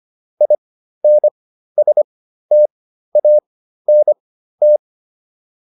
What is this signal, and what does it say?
Morse code, heard as a keyed tone, about 25 wpm, spaced as for 13 wpm, 600 Hz. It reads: INSTANT